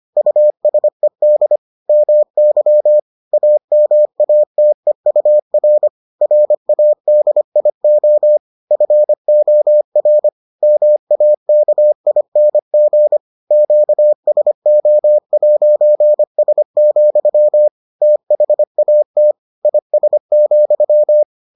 Morse code, heard as a keyed tone, about 25 wpm, 600 Hz. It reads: USED MY AMATEUR RADIO FOR MAKING QSO'S, THAT IS,